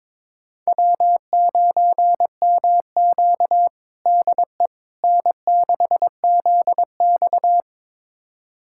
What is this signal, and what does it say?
Morse code, heard as a keyed tone, about 22 wpm, 700 Hz. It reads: W9MQ DE N6ZX
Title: USA Callsign DE Callsign - 4 Characters - 22wpm